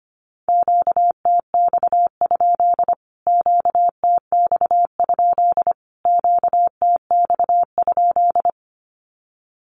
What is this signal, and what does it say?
Morse code, heard as a keyed tone, about 25 wpm, 700 Hz. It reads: QTX? QTX? QTX?
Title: Q-Codes Repeated 3x 25wpm